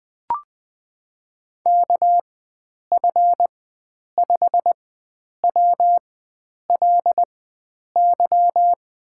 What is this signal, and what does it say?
Morse code, heard as a keyed tone, about 21 wpm, spaced as for 9 wpm, 700 Hz. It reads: KF5WLY
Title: USA Call Signs - 6 Characters - 4x Character Spacing - 20wpm